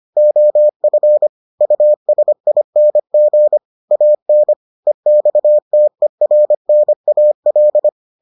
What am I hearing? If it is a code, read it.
Morse code, heard as a keyed tone, about 25 wpm, 600 Hz. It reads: OF USING AN EXTERNAL